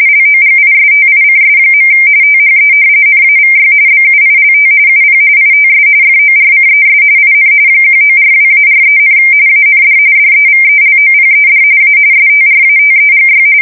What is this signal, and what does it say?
RTTY, about 45.45 baud. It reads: LOGISTICS. THE NAVY BUILT UP ITS FORCES IN 1942-43, AND DEVELOPED A STRATEGY
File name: fsk170 / usnavy4